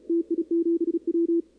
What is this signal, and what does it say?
Morse code, heard as a keyed tone, about 37 wpm, 340 Hz, with radio noise. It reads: TI7W
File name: contests / 2019-cw-wpx-kt5j-k5pi- -2330 0-11